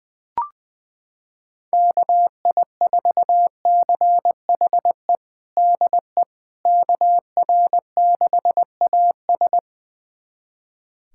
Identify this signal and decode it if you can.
Morse code, heard as a keyed tone, about 20 wpm, 700 Hz. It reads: KI4CHE DE KR6AS
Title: USA Callsign DE Callsign 20wpm